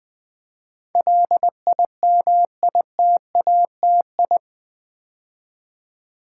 Morse code, heard as a keyed tone, about 20 wpm, 700 Hz. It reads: LIMITATI
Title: Build A Word - 4 Letters At A Time - 10-Letter Words - 20wpm